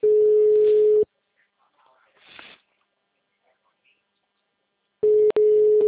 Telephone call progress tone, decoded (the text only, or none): ringback tone